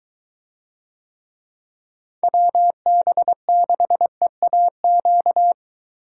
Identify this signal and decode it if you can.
Morse code, heard as a keyed tone, about 22 wpm, 700 Hz. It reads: WB6EAQ